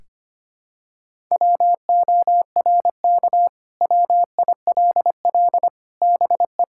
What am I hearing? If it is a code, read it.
Morse code, heard as a keyed tone, about 25 wpm, 700 Hz. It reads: WORK WILL BE
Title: Sentences from Top 1000 Words - Review Part 6 - 25wpm